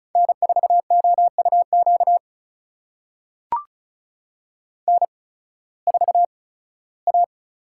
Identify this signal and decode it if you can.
Morse code, heard as a keyed tone, about 34 wpm, spaced as for 18 wpm, 700 Hz. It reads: N4OUQ N4A
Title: USA Call Signs - 5 Characters - 8x Character Spacing - 35wpm